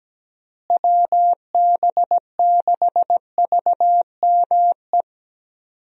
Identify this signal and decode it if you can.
Morse code, heard as a keyed tone, about 17 wpm, 700 Hz. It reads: WB6VME
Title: USA Call Signs - 6 Characters - Repeated 2x - 17wpm